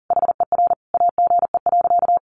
Morse code, heard as a keyed tone, about 40 wpm, 700 Hz. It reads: HER AGE.
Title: Sentences from Top 1300 Words 40wpm